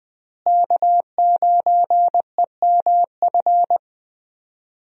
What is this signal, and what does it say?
Morse code, heard as a keyed tone, about 20 wpm, 700 Hz. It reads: K9EMF
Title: USA Call Signs - 5 Characters - Repeated 2x - 20wpm